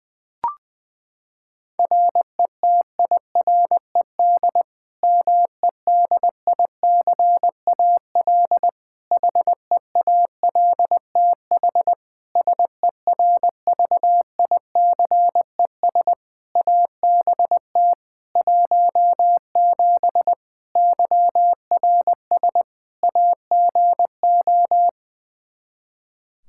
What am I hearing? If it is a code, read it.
Morse code, heard as a keyed tone, about 21 wpm, 700 Hz. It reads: RETIRED MEDICAL HEALTH SERVICES ABT 17 YRS AGO
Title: K1USN SST QSO Practice 20wpm